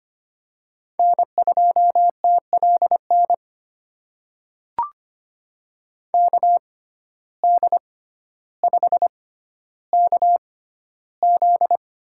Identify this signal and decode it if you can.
Morse code, heard as a keyed tone, about 25 wpm, spaced as for 13 wpm, 700 Hz. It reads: N2TLN KD5KZ
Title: USA Call Signs - 6x Character Spacing - 25wpm